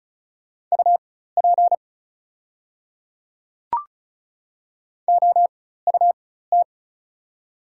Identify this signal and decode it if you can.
Morse code, heard as a keyed tone, about 36 wpm, spaced as for 16 wpm, 690 Hz. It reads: UP OUT
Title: 4x Character Spacing - Top 200 Words - 35wpm